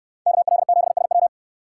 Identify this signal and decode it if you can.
Morse code, heard as a keyed tone, about 68 wpm, 700 Hz. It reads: KL7SW